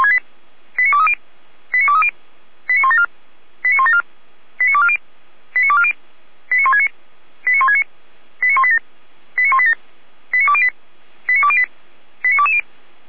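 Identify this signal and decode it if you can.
Five-tone selective call (ZVEI), heard as quick runs of five tones, tones of about 70 ms. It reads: 89230 89230 89174 89174 89240 89240 89169 89169 89187 89187 89208 89208 89220
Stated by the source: ZVEI five-tone call received by SDR, archived by fire brigade Feuerwehr Orlamünde.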